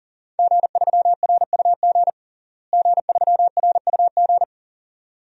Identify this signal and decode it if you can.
Morse code, heard as a keyed tone, about 40 wpm, 700 Hz. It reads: G3RUG G3RUG
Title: DXCC-100 Call Signs - 5 Characters - Repeated 2x - 40wpm